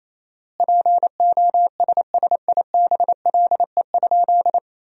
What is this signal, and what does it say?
Morse code, heard as a keyed tone, about 28 wpm, 700 Hz. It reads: POSSIBLE?
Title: 4x Word-Spacing - Sentences from Top 500 Words - 28wpm